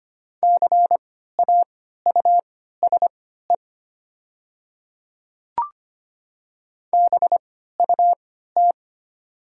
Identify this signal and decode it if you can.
Morse code, heard as a keyed tone, about 25 wpm, spaced as for 14 wpm, 700 Hz. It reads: CAUSE BUT
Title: 3x Character Spacing - Top 200 Words - 25wpm